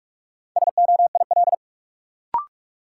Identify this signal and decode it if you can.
Morse code, heard as a keyed tone, about 47 wpm, 700 Hz. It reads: SOIL
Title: Sentences from Top 1500 Words - New Word Drill Alt - 45wpm